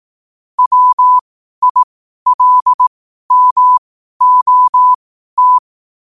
Morse code, heard as a keyed tone, about 18 wpm, spaced as for 12 wpm, 1000 Hz. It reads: WILMOT
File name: 12wpm